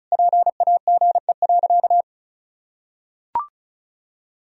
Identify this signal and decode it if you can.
Morse code, heard as a keyed tone, about 36 wpm, 700 Hz. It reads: PAGE.
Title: Sentences from Top 1000 Words - Review Part 1 - 35wpm